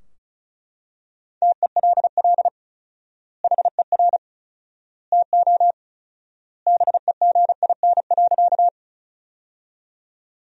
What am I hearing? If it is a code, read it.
Morse code, heard as a keyed tone, about 34 wpm, 700 Hz. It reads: TELL HER TO BEGIN.
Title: 4x Word-Spacing - Sentences from Top 300 Words - 35wpm